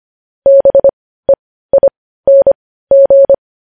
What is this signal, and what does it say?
Morse code, heard as a keyed tone, about 25 wpm, spaced as for 15 wpm, 550 Hz. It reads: BEING
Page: news headlines in morse code at 15 wpm